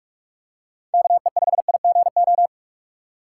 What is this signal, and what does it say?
Morse code, heard as a keyed tone, about 44 wpm, 700 Hz. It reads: KE5IGO
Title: USA Call Signs - 6 Characters - 6x Character Spacing - 45wpm